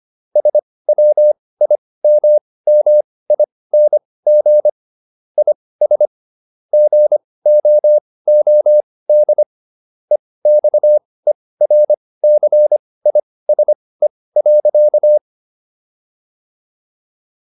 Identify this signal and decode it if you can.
Morse code, heard as a keyed tone, about 25 wpm, spaced as for 18 wpm, 600 Hz. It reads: SWIMMING IS GOOD EXERCISE.
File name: SS108 18